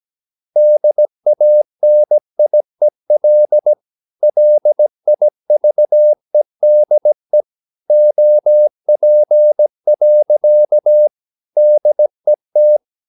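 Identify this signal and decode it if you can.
Morse code, heard as a keyed tone, about 17 wpm, 600 Hz. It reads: DANIEL LIVEDE OP. DET
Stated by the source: Slow (QRS) Morse practice at 17 wpm, from CW med Gnister.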